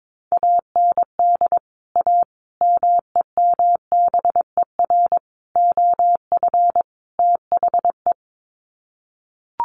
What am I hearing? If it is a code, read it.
Morse code, heard as a keyed tone, about 22 wpm, 700 Hz. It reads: AND A MEMBER OF THE
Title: Sets of 5 Words 22wpm